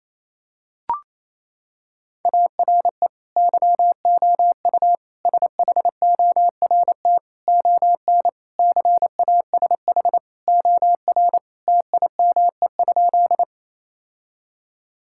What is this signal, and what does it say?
Morse code, heard as a keyed tone, about 28 wpm, 700 Hz. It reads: ARE YOU SHORT ON CASH OR TIME?
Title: Sentences from Top 1000 Words - Review Part 5 - 28wpm